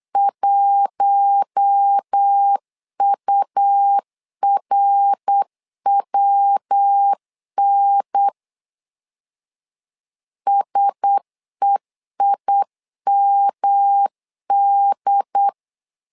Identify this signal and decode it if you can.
Morse code, heard as a keyed tone, about 8 wpm, 800 Hz. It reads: 1URWN SEIMD
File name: LEZIONI / 4 Q1S 8wpm 2710